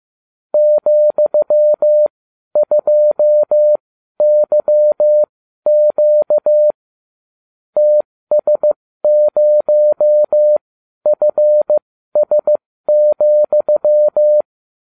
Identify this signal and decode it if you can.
Morse code, heard as a keyed tone, about 15 wpm, spaced as for 12 wpm, 600 Hz. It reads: ,2YQ TS0FS,